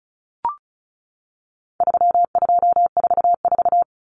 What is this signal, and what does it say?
Morse code, heard as a keyed tone, about 34 wpm, 700 Hz. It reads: 3244
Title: Common Cut Numbers - 4-Digit 35wpm